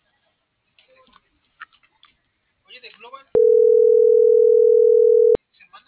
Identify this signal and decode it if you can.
Ringback tone (North America), 440 + 480 Hz, with speech.